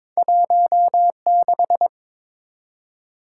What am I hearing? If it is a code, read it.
Morse code, heard as a keyed tone, about 23 wpm, 700 Hz. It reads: 16
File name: practice / 22wpm / Two Letter-Number Combinations - 22wpm